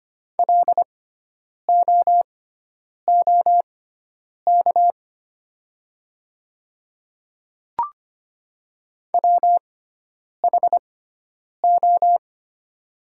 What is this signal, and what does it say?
Morse code, heard as a keyed tone, about 25 wpm, spaced as for 9 wpm, 700 Hz. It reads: LOOK WHO
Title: 6x Character Spacing - Top 100 Words - 25wpm